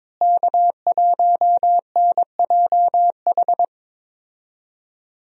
Morse code, heard as a keyed tone, about 21 wpm, 700 Hz. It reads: K1NJH
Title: USA Call Signs - 5 Characters - Repeated 3x - 22wpm